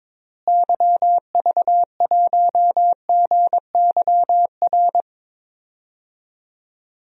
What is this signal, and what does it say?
Morse code, heard as a keyed tone, about 22 wpm, 700 Hz. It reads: YV1GYR
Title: DXCC-100 Call Signs - 6 Characters - Repeated 2x - 22wpm